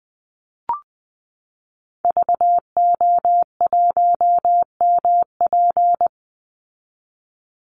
Morse code, heard as a keyed tone, about 20 wpm, 700 Hz. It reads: VO1MP
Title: Call Signs Repeated 3x - 20wpm